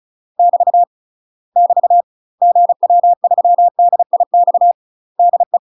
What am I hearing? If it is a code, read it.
Morse code, heard as a keyed tone, about 34 wpm, 700 Hz. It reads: <BT> <BT> GW3DIX DE
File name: UK MASTER QSOs TRAINING / 35WPM / UK MASTER QSOs 0016